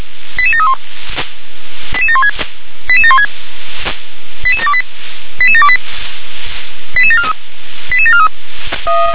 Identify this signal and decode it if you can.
Five-tone selective call (ZVEI), heard as quick runs of five tones, tones of about 70 ms. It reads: 88731 88716 88716 88728 88728 88743 88743
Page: /357/SDR/ZVEI/88730/